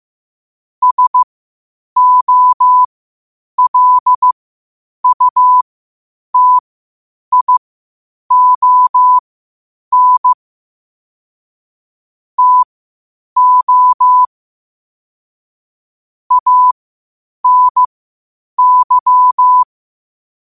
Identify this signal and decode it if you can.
Morse code, heard as a keyed tone, about 15 wpm, spaced as for 8 wpm, 1000 Hz. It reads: SOLUTION TO ANY
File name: qotdfiles / 8wpm